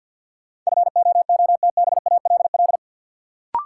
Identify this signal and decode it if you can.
Morse code, heard as a keyed tone, about 52 wpm, 700 Hz. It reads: FOOTBALL
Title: Sentences from Top 1600 Words - New Word Drill - 50wpm